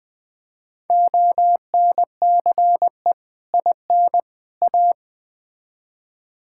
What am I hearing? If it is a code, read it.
Morse code, heard as a keyed tone, about 20 wpm, 700 Hz. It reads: ONCE IN A
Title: Build A Phrase - 4-Word Phrases - 20wpm